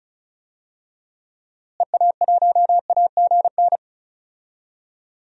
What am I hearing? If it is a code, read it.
Morse code, heard as a keyed tone, about 36 wpm, 700 Hz. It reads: EA1AGN
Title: DXCC-100 Call Signs - 6 Characters - Repeated 3x - 35wpm